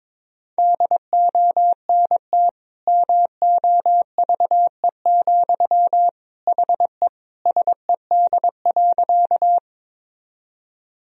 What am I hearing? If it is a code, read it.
Morse code, heard as a keyed tone, about 22 wpm, 700 Hz. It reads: DONT MOVE, HE SED.
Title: Sentences from Top 300 Words and Abbreviations 22wpm